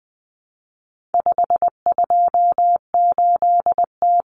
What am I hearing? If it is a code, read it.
Morse code, heard as a keyed tone, about 20 wpm, 700 Hz. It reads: 528T
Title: Common Cut Numbers - 4-Digit 20wpm